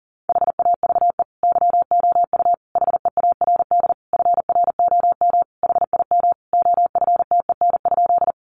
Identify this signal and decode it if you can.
Morse code, heard as a keyed tone, about 40 wpm, 700 Hz. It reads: HAVE YOU HEARD FROM HIM OFTEN?
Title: Sentences from Top 400 Words 40wpm